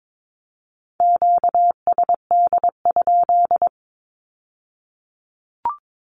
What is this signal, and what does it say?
Morse code, heard as a keyed tone, about 22 wpm, 700 Hz. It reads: QSD?
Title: Q-Codes with Spoken Meaning - 22wpm